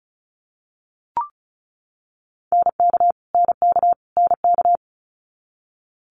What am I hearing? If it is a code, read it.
Morse code, heard as a keyed tone, about 36 wpm, 700 Hz. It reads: NK NK NK
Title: Multi-letter Phonemes Repeated 3x 35wpm